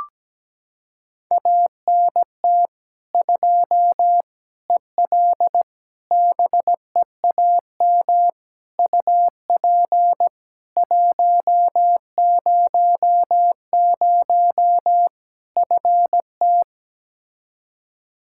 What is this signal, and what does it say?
Morse code, heard as a keyed tone, about 17 wpm, 700 Hz. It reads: ANT 2 EL BEAM UP 100 FT
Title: Practice QSO - Exchange 2 - Info - 17wpm